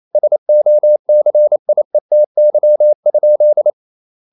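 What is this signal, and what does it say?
Morse code, heard as a keyed tone, about 28 wpm, 600 Hz. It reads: SOCIETY?
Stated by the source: Morse code practice at 28 wpm, text of Ken Follett's novel Never in English, from CW med Gnister.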